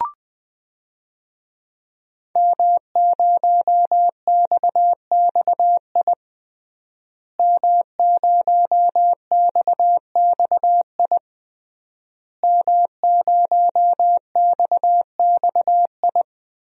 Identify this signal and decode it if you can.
Morse code, heard as a keyed tone, about 20 wpm, 700 Hz. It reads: M0XXI M0XXI M0XXI
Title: DXCC-100 Call Signs - 5 Characters - Repeated 3x - 20wpm